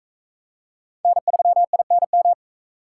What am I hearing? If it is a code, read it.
Morse code, heard as a keyed tone, about 36 wpm, 700 Hz. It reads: N3INM